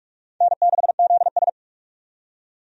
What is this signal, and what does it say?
Morse code, heard as a keyed tone, about 45 wpm, 700 Hz. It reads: N6ZS
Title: USA Call Signs - 4 Characters - 45wpm